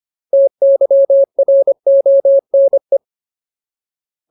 Morse code, heard as a keyed tone, about 25 wpm, 550 Hz. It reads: TYRONE
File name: Ultimate Word List / 25WPM Ultimate